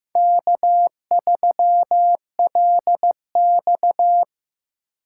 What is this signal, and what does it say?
Morse code, heard as a keyed tone, about 15 wpm, 700 Hz. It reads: K3LX
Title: USA Call Signs - 4 Characters - 6x Character Spacing - 15wpm